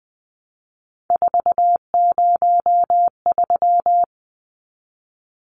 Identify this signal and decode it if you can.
Morse code, heard as a keyed tone, about 20 wpm, 700 Hz. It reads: 403
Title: Numbers - 3-Digit 20wpm